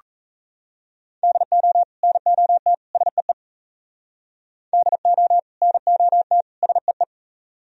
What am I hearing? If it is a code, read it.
Morse code, heard as a keyed tone, about 37 wpm, 700 Hz. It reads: DO NOT SEE DO NOT SEE